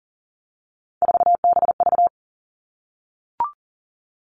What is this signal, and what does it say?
Morse code, heard as a keyed tone, about 39 wpm, 700 Hz. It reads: 4BV
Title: Three Challenging Letter-Number Combinations 40wpm